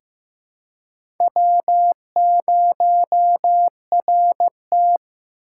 Morse code, heard as a keyed tone, about 15 wpm, 700 Hz. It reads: W0RT
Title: USA Call Signs - 4 Characters - 2x Character Spacing - 15wpm